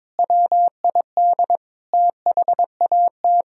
Morse code, heard as a keyed tone, about 22 wpm, 700 Hz. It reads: WID THAT
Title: Sentences from Top 300 Words and Abbreviations 22wpm